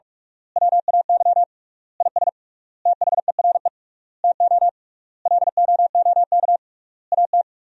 Morse code, heard as a keyed tone, about 46 wpm, 700 Hz. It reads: WAY IS THERE TO LOOK AT
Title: 3x Word-Spacing - Sentences from Top 100 Words - 45wpm